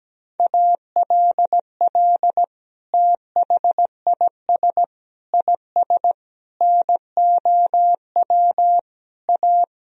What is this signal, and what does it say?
Morse code, heard as a keyed tone, about 17 wpm, 700 Hz. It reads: ALL THIS IS NOW A
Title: Sentences from Top 1000 Words - Review Part 3 - 17wpm